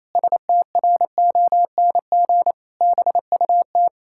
Morse code, heard as a keyed tone, about 28 wpm, 700 Hz. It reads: STRONG BUT
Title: Pangram Challenge v2 - 28wpm